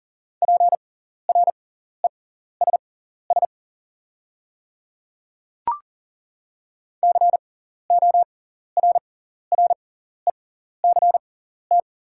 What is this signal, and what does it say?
Morse code, heard as a keyed tone, about 40 wpm, spaced as for 14 wpm, 690 Hz. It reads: PRESS CORRECT
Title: 6x Character Spacing - Top 500 Words - 40wpm